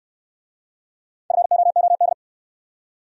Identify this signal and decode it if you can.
Morse code, heard as a keyed tone, about 69 wpm, 700 Hz. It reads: FOOD